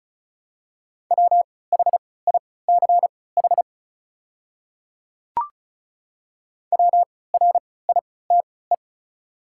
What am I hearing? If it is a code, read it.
Morse code, heard as a keyed tone, about 36 wpm, spaced as for 20 wpm, 690 Hz. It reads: WHICH WRITE
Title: 3x Character Spacing - Top 100 Words - 35wpm